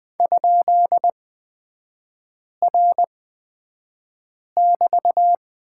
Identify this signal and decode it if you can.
Morse code, heard as a keyed tone, about 20 wpm, spaced as for 5 wpm, 690 Hz. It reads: ?R<BT>
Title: 229 Lesson 37e - Letter x - Three Characters 20wpm